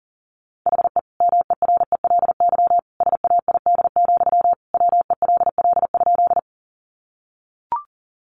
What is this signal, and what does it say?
Morse code, heard as a keyed tone, about 40 wpm, 700 Hz. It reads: HE MERELY SAID, WELL?
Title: Sentences from Top 1400 Words 40wpm